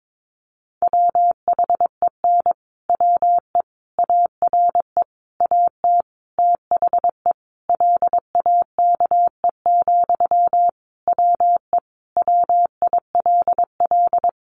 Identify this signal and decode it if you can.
Morse code, heard as a keyed tone, about 22 wpm, 700 Hz. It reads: WHEN WE ARE AT THE LAKE, WE WILL
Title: Sentences from Top 1100 Words 22wpm